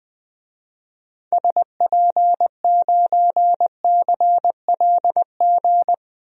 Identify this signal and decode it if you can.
Morse code, heard as a keyed tone, about 20 wpm, 700 Hz. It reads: SP9CLG
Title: DXCC-100 Call Signs - 6 Characters - 20wpm